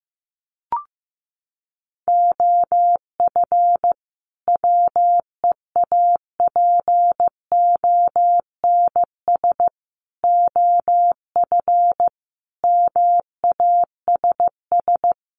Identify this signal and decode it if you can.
Morse code, heard as a keyed tone, about 15 wpm, 700 Hz. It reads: OF WEAPONS OF MASS
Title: Sets of 5 Words Encore 15wpm